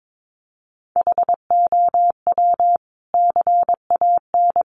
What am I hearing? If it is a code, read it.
Morse code, heard as a keyed tone, about 22 wpm, 700 Hz. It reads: HOW CAN
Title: Sentences from Top 1300 Words 22wpm